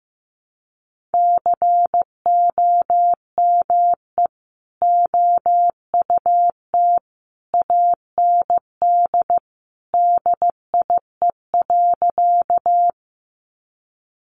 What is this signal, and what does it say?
Morse code, heard as a keyed tone, about 15 wpm, 700 Hz. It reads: COME OUT AND DIE.
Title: Sentences from Top 1300 Words 15wpm